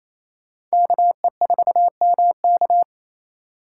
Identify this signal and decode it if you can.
Morse code, heard as a keyed tone, about 35 wpm, 700 Hz. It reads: KE4MK
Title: Speed Racing - USA 2x2 Call Signs 28wpm